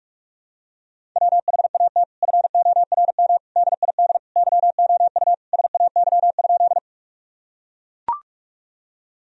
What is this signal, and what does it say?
Morse code, heard as a keyed tone, about 42 wpm, 700 Hz. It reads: WHAT FORM DID YOU SAY?
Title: Sentences from Top 1000 Words - Review Part 1 - 45wpm